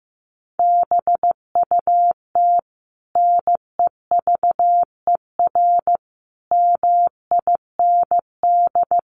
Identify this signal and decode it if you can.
Morse code, heard as a keyed tone, about 15 wpm, 700 Hz. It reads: BUT NEVER MIND
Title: Sentences from Top 500 Words 15wpm